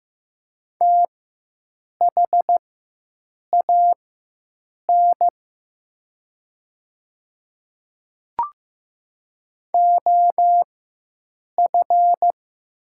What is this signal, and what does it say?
Morse code, heard as a keyed tone, about 15 wpm, spaced as for 7 wpm, 700 Hz. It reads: THAN OF
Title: 4x Character Spacing - Top 100 Words - 15wpm